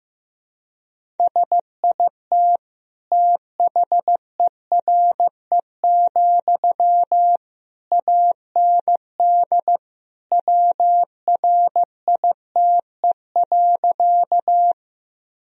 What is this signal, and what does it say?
Morse code, heard as a keyed tone, about 14 wpm, 700 Hz. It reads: SIT THERE, AND WRITE.
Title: Sentences from Top 1000 Words - Review Part 1 - 15wpm